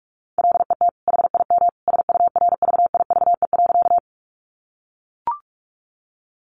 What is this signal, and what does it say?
Morse code, heard as a keyed tone, about 44 wpm, 700 Hz. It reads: LET HIM SURVIVE.
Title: Sentences from Top 1300 Words 45wpm